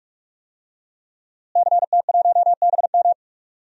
Morse code, heard as a keyed tone, about 43 wpm, 700 Hz. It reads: CT1BM
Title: DXCC-100 Call Signs - 5 Characters - Repeated 2x - 45wpm